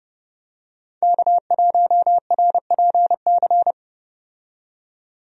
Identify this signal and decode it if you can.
Morse code, heard as a keyed tone, about 31 wpm, 700 Hz. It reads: K1RPC
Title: USA Call Signs - 5 Characters - 30wpm